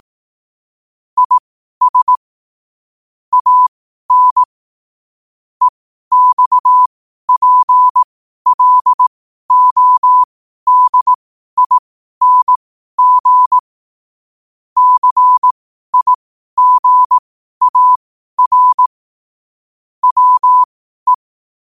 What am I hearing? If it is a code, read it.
Morse code, heard as a keyed tone, about 18 wpm, spaced as for 12 wpm, 1000 Hz. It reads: IS AN EXPLODING CIGAR WE
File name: qotdfiles / 12wpm